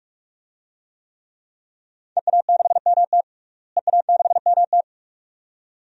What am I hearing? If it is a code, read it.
Morse code, heard as a keyed tone, about 42 wpm, 700 Hz. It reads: EA6MT EA6MT
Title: DXCC-100 Call Signs - 5 Characters - Repeated 2x - 45wpm